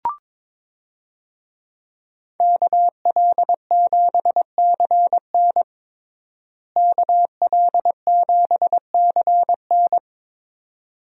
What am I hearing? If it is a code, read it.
Morse code, heard as a keyed tone, about 22 wpm, 700 Hz. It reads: KL7CN KL7CN
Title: USA Call Signs - 5 Characters - Repeated 2x - 22wpm